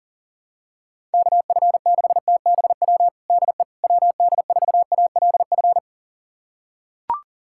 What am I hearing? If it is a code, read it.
Morse code, heard as a keyed tone, about 40 wpm, 700 Hz. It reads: KF6TBW DE WD4ALF
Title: USA Callsign DE Callsign - 6 Characters - 40wpm